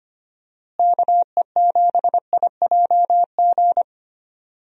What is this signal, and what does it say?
Morse code, heard as a keyed tone, about 25 wpm, 700 Hz. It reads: KE7IJG
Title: USA Call Signs - 6 Characters - Repeated 2x - 25wpm